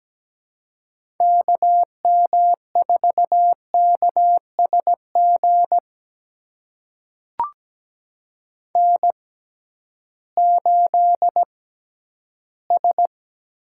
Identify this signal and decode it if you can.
Morse code, heard as a keyed tone, about 17 wpm, spaced as for 11 wpm, 700 Hz. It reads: KM4KSG N8S
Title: USA Call Signs - 6x Character Spacing - 17wpm